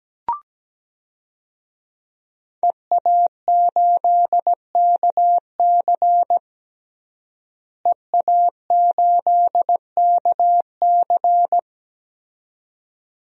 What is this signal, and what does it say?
Morse code, heard as a keyed tone, about 17 wpm, 700 Hz. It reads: EA8KC EA8KC
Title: DXCC-100 Call Signs - 5 Characters - Repeated 2x - 17wpm